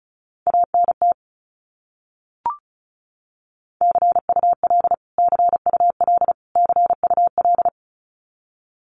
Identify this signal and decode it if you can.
Morse code, heard as a keyed tone, about 34 wpm, 700 Hz. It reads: ANT CUL CUL CUL
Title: Abbreviations Repeated 3x - 35wpm